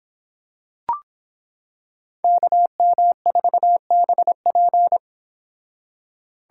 Morse code, heard as a keyed tone, about 26 wpm, 700 Hz. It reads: KM4BP